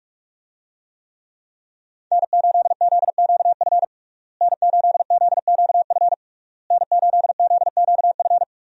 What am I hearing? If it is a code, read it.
Morse code, heard as a keyed tone, about 42 wpm, 700 Hz. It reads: N8ZQF N8ZQF N8ZQF
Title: USA Call Signs - 5 Characters - Repeated 3x - 45wpm